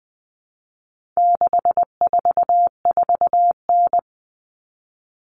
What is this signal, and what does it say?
Morse code, heard as a keyed tone, about 20 wpm, 700 Hz. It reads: 644N
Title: All Cut Numbers - 4-Digit 20wpm